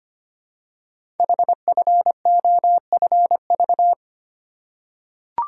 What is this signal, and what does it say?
Morse code, heard as a keyed tone, about 25 wpm, 700 Hz. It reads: HFOFV
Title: Code Groups - 5 Letters - 25wpm